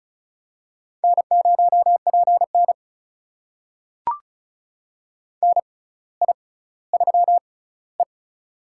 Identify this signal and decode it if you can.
Morse code, heard as a keyed tone, about 36 wpm, spaced as for 18 wpm, 700 Hz. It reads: N0PN NI3E
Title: USA Call Signs - 4 Characters - 6x Character Spacing - 35wpm